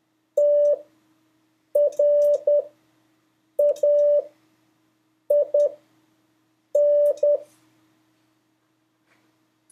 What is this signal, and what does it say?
Morse code, heard as a keyed tone, about 10 wpm, spaced as for 6 wpm, 580 Hz. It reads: TRAIN
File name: 2of5-words-1